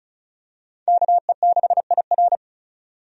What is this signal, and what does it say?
Morse code, heard as a keyed tone, about 36 wpm, 700 Hz. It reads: KE6IR
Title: USA Call Signs - 5 Characters - 35wpm